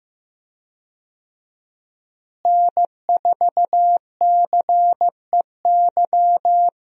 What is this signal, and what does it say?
Morse code, heard as a keyed tone, about 15 wpm, 700 Hz. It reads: N4CEY
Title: USA Call Signs - 5 Characters - Repeated 2x - 15wpm